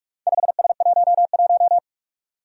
Morse code, heard as a keyed tone, about 44 wpm, 700 Hz. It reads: 5S11
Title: Four Challenging Letter-Number Combinations 45wpm